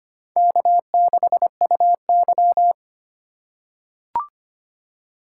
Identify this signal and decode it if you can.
Morse code, heard as a keyed tone, about 25 wpm, 700 Hz. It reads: K6UY
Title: USA Call Signs - 4 Characters - 4x Character Spacing - 25wpm